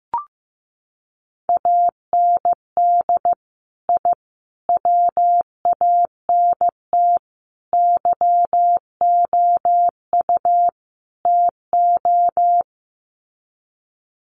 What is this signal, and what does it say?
Morse code, heard as a keyed tone, about 15 wpm, 700 Hz. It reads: AND I WANT YOU TO
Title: Sets of 5 Words Encore 15wpm